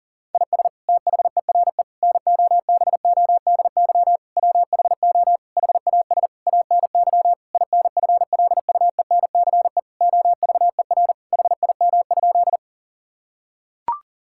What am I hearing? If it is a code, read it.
Morse code, heard as a keyed tone, about 40 wpm, 700 Hz. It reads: IS THERE NOBODY WHO HAS ANY INFLUENCE OVER HIM?
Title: Sentences from Top 1600 Words 40wpm